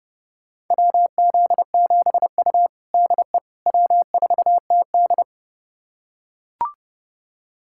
Morse code, heard as a keyed tone, about 30 wpm, 700 Hz. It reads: WZ7U DE W4TD
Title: USA Callsign DE Callsign - 4 Characters - 30wpm